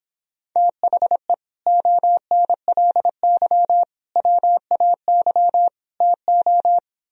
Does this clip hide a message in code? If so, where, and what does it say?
Yes: Morse code from 0.6 to 6.8 s, heard as a keyed tone, about 26 wpm, 700 Hz. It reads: THE ONLY WAY TO